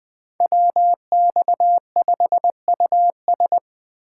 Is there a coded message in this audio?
Yes: Morse code from 0.4 to 3.6 s, heard as a keyed tone, about 21 wpm, 700 Hz. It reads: WX5US